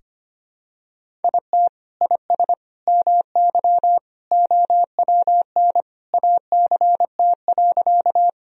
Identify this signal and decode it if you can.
Morse code, heard as a keyed tone, about 25 wpm, 700 Hz. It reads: IT IS MY OWN ACT.
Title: Sentences from Top 1000 Words - Review Part 1 - 25wpm